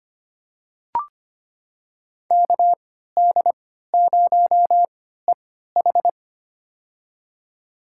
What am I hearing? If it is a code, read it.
Morse code, heard as a keyed tone, about 25 wpm, spaced as for 14 wpm, 700 Hz. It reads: KD0EH
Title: USA Call Signs - 5 Characters - 3x Character Spacing - 25wpm